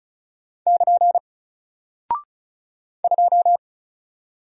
Morse code, heard as a keyed tone, about 36 wpm, 700 Hz. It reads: <KN> 2
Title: Single Character (Letter:Number:Punctuation:Prosign) 35wpm